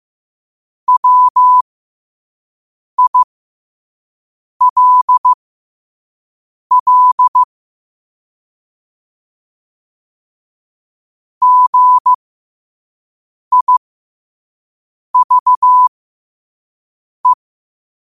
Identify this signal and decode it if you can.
Morse code, heard as a keyed tone, about 15 wpm, spaced as for 5 wpm, 1000 Hz. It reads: WILL GIVE
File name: qotdfiles / 5wpm